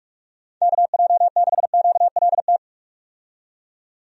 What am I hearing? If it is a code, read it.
Morse code, heard as a keyed tone, about 43 wpm, 700 Hz. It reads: KJ6QLT
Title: USA Call Signs - 6 Characters - 45wpm